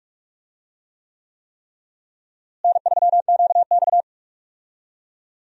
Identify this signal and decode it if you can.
Morse code, heard as a keyed tone, about 42 wpm, 700 Hz. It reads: N3QX
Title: USA Call Signs - 4 Characters - Repeated 3x - 45wpm